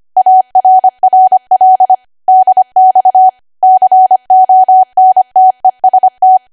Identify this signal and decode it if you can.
Morse code, heard as a keyed tone, about 25 wpm, 750 Hz. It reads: ARRL DX CONTEST
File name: cwa / W1AW-Level3 / 25WPM